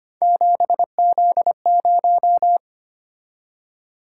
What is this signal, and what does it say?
Morse code, heard as a keyed tone, about 25 wpm, 700 Hz. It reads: 7Z0
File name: Three Letter-Number Combinations - 25wpm